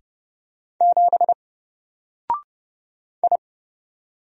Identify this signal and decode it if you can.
Morse code, heard as a keyed tone, about 30 wpm, 700 Hz. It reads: 7 I
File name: Single-Letter-Number-30wpm